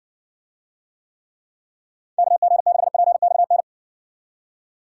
Morse code, heard as a keyed tone, about 60 wpm, 700 Hz. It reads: KG6PXN